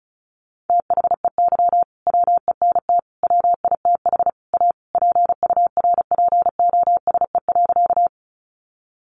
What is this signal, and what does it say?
Morse code, heard as a keyed tone, about 34 wpm, 700 Hz. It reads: THEY WENT WITH A PURPOSE.
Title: Sentences from Top 800 Words 35wpm